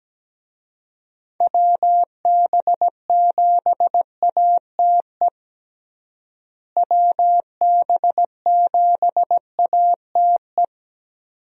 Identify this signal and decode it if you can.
Morse code, heard as a keyed tone, about 17 wpm, 700 Hz. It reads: WB7ATE WB7ATE
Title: USA Call Signs - 6 Characters - Repeated 2x - 17wpm